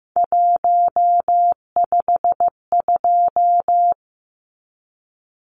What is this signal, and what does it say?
Morse code, heard as a keyed tone, about 15 wpm, 700 Hz. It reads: 152
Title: Numbers - 3-Digit 15wpm